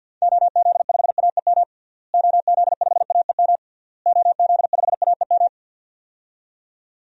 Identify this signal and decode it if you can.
Morse code, heard as a keyed tone, about 52 wpm, 700 Hz. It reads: OZ5AEM OZ5AEM OZ5AEM
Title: DXCC-100 Call Signs - 6 Characters - Repeated 3x - 50wpm